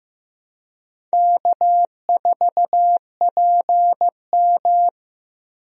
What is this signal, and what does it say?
Morse code, heard as a keyed tone, about 15 wpm, 700 Hz. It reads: K4PM
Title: USA Call Signs - 4 Characters - Repeated 3x - 15wpm